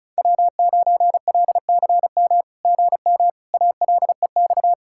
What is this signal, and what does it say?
Morse code, heard as a keyed tone, about 33 wpm, 700 Hz. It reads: W9LCM GM ALEX
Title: SOTA QSO Practice 35wpm